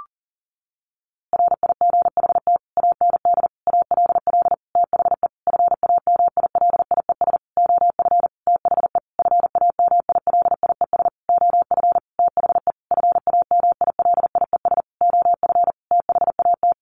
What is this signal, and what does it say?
Morse code, heard as a keyed tone, about 40 wpm, 700 Hz. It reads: RIGHT AND ALL THE FAMILIES OF THE FAMILIES OF THE FAMILIES OF THAT
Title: A Sound Of Thunder 40wpm